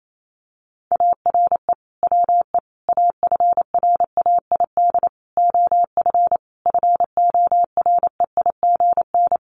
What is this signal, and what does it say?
Morse code, heard as a keyed tone, about 28 wpm, 700 Hz. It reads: ARE WE AFRAID OF FOREIGN
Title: Sentences from Top 1500 Words 28wpm